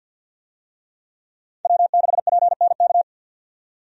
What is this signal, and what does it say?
Morse code, heard as a keyed tone, about 50 wpm, 700 Hz. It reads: W6PNK
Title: USA Call Signs - 5 Characters - 50wpm